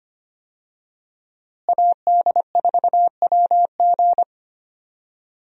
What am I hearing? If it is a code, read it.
Morse code, heard as a keyed tone, about 25 wpm, 700 Hz. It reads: AD4WG
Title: USA Call Signs - 5 Characters - Repeated 2x - 25wpm